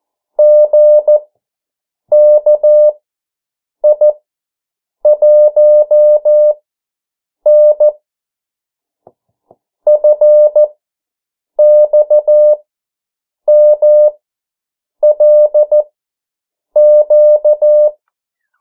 Morse code, heard as a keyed tone, about 14 wpm, spaced as for 7 wpm, 600 Hz. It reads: GKI1N FXMLQ
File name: morse mp3 / 09 urok 8